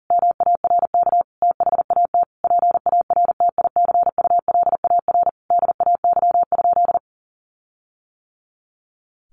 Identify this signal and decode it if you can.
Morse code, heard as a keyed tone, about 40 wpm, 700 Hz. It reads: MARK THAT PARTICULAR DAY?
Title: Sentences from Top 1400 Words 40wpm